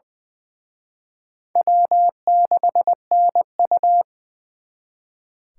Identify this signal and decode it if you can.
Morse code, heard as a keyed tone, about 24 wpm, 700 Hz. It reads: W6NU
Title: Speed Racing - USA 1x2 Call Signs 20wpm